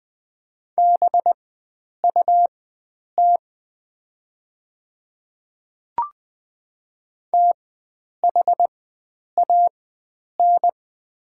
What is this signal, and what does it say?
Morse code, heard as a keyed tone, about 20 wpm, spaced as for 9 wpm, 690 Hz. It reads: BUT THAN
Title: 4x Character Spacing - Top 100 Words - 20wpm